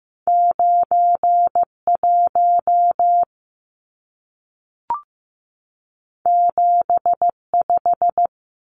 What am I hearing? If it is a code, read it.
Morse code, heard as a keyed tone, about 15 wpm, 700 Hz. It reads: 91 75
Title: Numbers - 2-Digit 15wpm